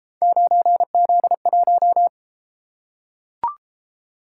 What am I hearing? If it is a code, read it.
Morse code, heard as a keyed tone, about 32 wpm, 700 Hz. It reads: 9Z1